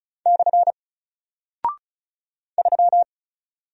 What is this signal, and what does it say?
Morse code, heard as a keyed tone, about 36 wpm, 700 Hz. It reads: / 3
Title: Single Character (Letter:Number:Punctuation:Prosign) Alt 35wpm